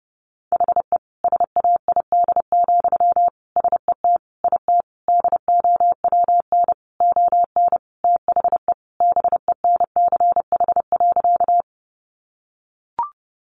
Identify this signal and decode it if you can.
Morse code, heard as a keyed tone, about 30 wpm, 700 Hz. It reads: HE SAID, SET IT DOWN ON THE BENCH.
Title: Sentences from Top 1400 Words 30wpm